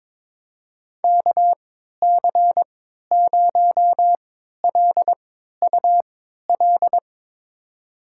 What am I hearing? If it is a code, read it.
Morse code, heard as a keyed tone, about 22 wpm, spaced as for 13 wpm, 700 Hz. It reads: KC0LUL
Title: USA Call Signs - 6 Characters - 3x Character Spacing - 22wpm